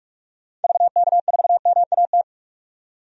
Morse code, heard as a keyed tone, about 43 wpm, 700 Hz. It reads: VK4MAT
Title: DXCC-100 Call Signs - 6 Characters - 45wpm